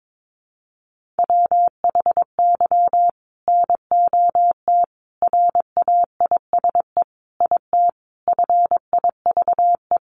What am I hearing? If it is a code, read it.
Morse code, heard as a keyed tone, about 22 wpm, 700 Hz. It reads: WHY NOT RAISE IT FIVE
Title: Sentences from Top 1400 Words 22wpm